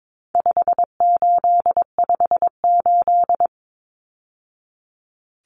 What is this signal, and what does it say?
Morse code, heard as a keyed tone, about 22 wpm, 700 Hz. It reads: 5858
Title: Numbers - 4-Digit 22wpm